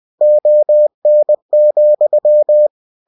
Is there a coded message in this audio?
Yes: Morse code from 0.2 to 2.7 s, heard as a keyed tone, about 20 wpm, 600 Hz. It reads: ON,